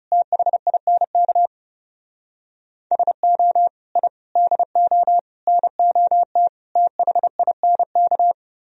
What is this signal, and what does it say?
Morse code, heard as a keyed tone, about 38 wpm, 700 Hz. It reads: THINK SO I DO NOT THINK